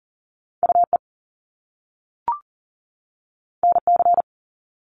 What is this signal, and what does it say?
Morse code, heard as a keyed tone, about 40 wpm, 700 Hz. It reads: UE NC
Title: Top Two Letter Combinations in Words 40wpm